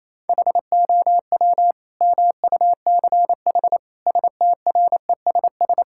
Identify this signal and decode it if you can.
Morse code, heard as a keyed tone, about 27 wpm, 700 Hz. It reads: HOW MUCH STRESS
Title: Sentences from Top 1000 Words - Review Part 6 - 28wpm